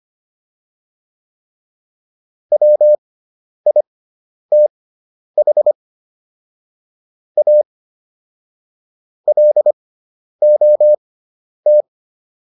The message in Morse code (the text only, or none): WITH A LOT